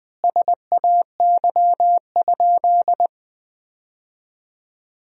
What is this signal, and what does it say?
Morse code, heard as a keyed tone, about 21 wpm, 700 Hz. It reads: SAY?
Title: 3x Word-Spacing - Sentences from Top 400 Words - 20wpm